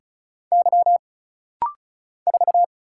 Morse code, heard as a keyed tone, about 36 wpm, 700 Hz. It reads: Y 4
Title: Single Letter-Number - Rapid-Fire 35wpm